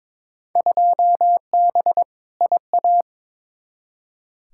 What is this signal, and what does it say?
Morse code, heard as a keyed tone, about 22 wpm, 700 Hz. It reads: 2B IA
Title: Field Day Exchanges Encore - 22wpm